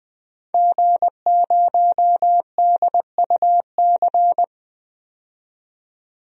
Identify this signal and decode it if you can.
Morse code, heard as a keyed tone, about 20 wpm, 700 Hz. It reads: G0DUC
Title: DXCC-100 Call Signs - 5 Characters - 20wpm